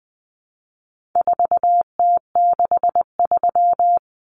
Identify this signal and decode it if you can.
Morse code, heard as a keyed tone, about 20 wpm, 700 Hz. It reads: 4T63
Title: Common Cut Numbers - 4-Digit 20wpm